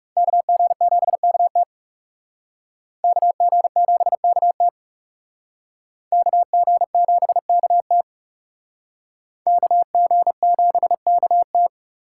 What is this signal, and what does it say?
Morse code, heard as a keyed tone, about 37 wpm, 700 Hz. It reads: KG7KT KG7KT KG7KT KG7KT
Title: Speed Racing - USA 2x2 Call Signs 30wpm